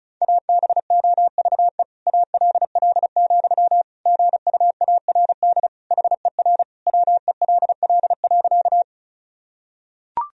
Morse code, heard as a keyed tone, about 36 wpm, 700 Hz. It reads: ABOVE ALL, GUARD HER WELL.
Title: Sentences from Top 1000 Words - Review Part 4 - 35wpm